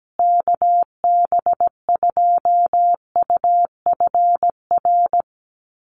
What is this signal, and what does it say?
Morse code, heard as a keyed tone, about 17 wpm, 700 Hz. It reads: KB2UFR
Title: USA Call Signs Repeated 3x 17wpm